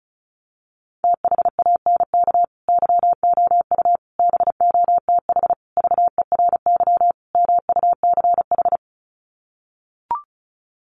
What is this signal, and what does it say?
Morse code, heard as a keyed tone, about 34 wpm, 700 Hz. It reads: THANK YOU BOTH VERY MUCH
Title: Sets of 5 Words 35wpm